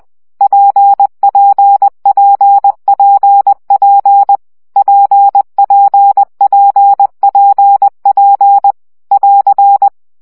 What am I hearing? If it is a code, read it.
Morse code, heard as a keyed tone, about 20 wpm, 800 Hz. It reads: PPPPP PPPPP <AR>
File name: CW gyakorlas p 17 baud